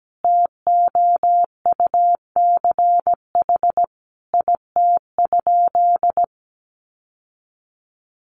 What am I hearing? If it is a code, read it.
Morse code, heard as a keyed tone, about 17 wpm, 700 Hz. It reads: TOUCH IT?
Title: Sentences from Top 1200 Words 17wpm